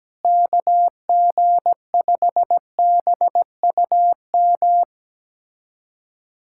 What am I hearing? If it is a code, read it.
Morse code, heard as a keyed tone, about 17 wpm, 700 Hz. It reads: KG5BUM
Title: USA Callsign DE Callsign - 6 Characters - 17wpm